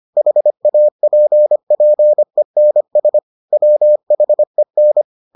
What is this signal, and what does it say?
Morse code, heard as a keyed tone, about 25 wpm, 600 Hz. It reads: HAPPENS WHEN